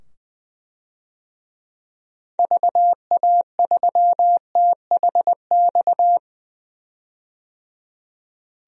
Morse code, heard as a keyed tone, about 20 wpm, 700 Hz. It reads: VA3THX
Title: DXCC-100 Call Signs - 6 Characters - 20wpm